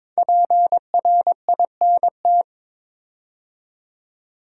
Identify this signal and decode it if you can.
Morse code, heard as a keyed tone, about 22 wpm, 700 Hz. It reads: PRINT
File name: 5 Letter Words - First 3 Letters Given - 22wpm